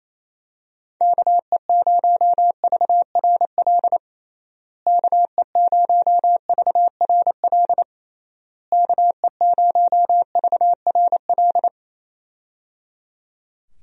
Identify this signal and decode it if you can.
Morse code, heard as a keyed tone, about 28 wpm, 700 Hz. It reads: KE0VRL KE0VRL KE0VRL
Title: USA Call Signs - 6 Characters - Repeated 3x - 28wpm